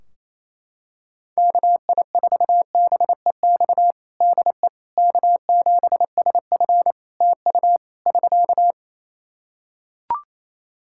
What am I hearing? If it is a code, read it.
Morse code, heard as a keyed tone, about 28 wpm, 700 Hz. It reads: KI4BEX DE K7SF TU <SK>
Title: Practice QSO - Exchange 4 - Ending - 28wpm